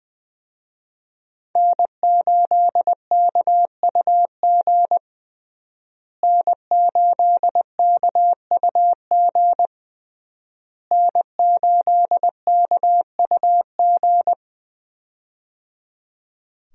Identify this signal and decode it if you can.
Morse code, heard as a keyed tone, about 20 wpm, 700 Hz. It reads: N8KUG N8KUG N8KUG
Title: USA Call Signs - 5 Characters - Repeated 3x - 20wpm